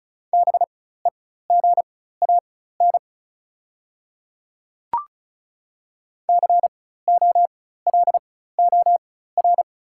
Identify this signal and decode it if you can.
Morse code, heard as a keyed tone, about 36 wpm, spaced as for 16 wpm, 690 Hz. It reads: BEGAN COLOR
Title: 4x Character Spacing - Top 300 Words - 35wpm